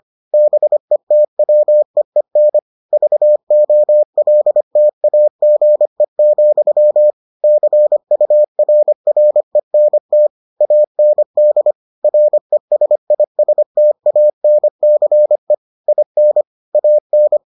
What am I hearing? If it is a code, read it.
Morse code, heard as a keyed tone, about 25 wpm, 600 Hz. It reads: BETWEEN VOLTAGE, CURRENT AND RESISTANCE IN AN